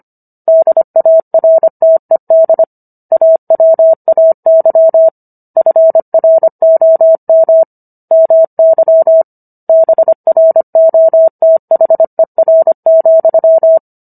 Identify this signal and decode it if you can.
Morse code, heard as a keyed tone, about 25 wpm, 650 Hz. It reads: DARTED AWAY FROM MY BROTHER,